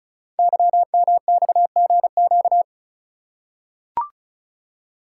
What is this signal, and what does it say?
Morse code, heard as a keyed tone, about 34 wpm, 700 Hz. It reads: YMXGQ
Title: Code Groups - 5 Letters - 35wpm